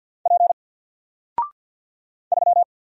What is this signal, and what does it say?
Morse code, heard as a keyed tone, about 50 wpm, 700 Hz. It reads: P 3
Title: Single Letter-Number 50wpm